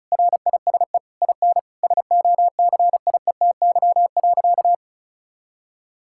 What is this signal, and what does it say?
Morse code, heard as a keyed tone, about 34 wpm, 700 Hz. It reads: RISE IN SOCIETY.
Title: Sentences from Top 1000 Words - Review Part 6 - 35wpm